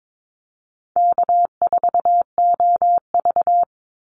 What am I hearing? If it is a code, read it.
Morse code, heard as a keyed tone, about 22 wpm, 700 Hz. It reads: K4OV
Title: Easy Call Signs Repeated 3x - 22wpm